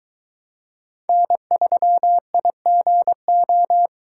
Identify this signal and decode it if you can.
Morse code, heard as a keyed tone, about 22 wpm, 700 Hz. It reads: N3IGO